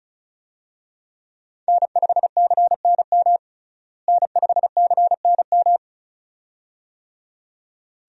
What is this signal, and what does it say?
Morse code, heard as a keyed tone, about 35 wpm, 700 Hz. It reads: N5CNM N5CNM
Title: USA Call Signs - 5 Characters - Repeated 2x - 35wpm